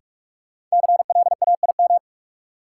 Morse code, heard as a keyed tone, about 46 wpm, 700 Hz. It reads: CLAIM
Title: Sentences from Top 1500 Words - New Word Drill Alt - 45wpm